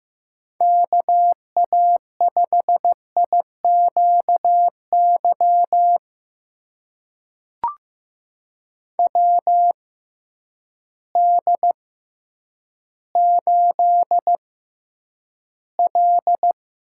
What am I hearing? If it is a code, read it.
Morse code, heard as a keyed tone, about 16 wpm, spaced as for 9 wpm, 700 Hz. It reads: KA5IQY WD8L
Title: USA Call Signs - 6 Characters - 6x Character Spacing - 15wpm